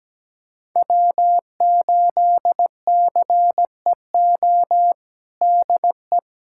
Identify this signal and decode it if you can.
Morse code, heard as a keyed tone, about 17 wpm, 700 Hz. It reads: W8CEO DE
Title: USA Callsign DE Callsign 17wpm